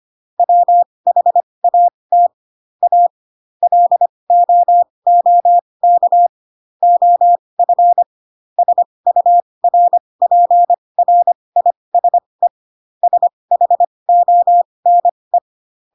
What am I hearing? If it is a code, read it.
Morse code, heard as a keyed tone, about 25 wpm, spaced as for 20 wpm, 700 Hz. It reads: WHAT A LOOK OF SURPRISE SHONE
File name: Pinocchio / Pinocchio ch1 15 25 ik1hge